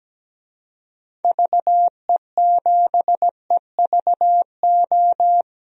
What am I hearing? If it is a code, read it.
Morse code, heard as a keyed tone, about 17 wpm, 700 Hz. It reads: VE7EVO
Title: DXCC-100 Call Signs - 6 Characters - Repeated 2x - 17wpm